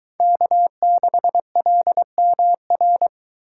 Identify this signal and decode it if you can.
Morse code, heard as a keyed tone, about 22 wpm, 700 Hz. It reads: K6LMR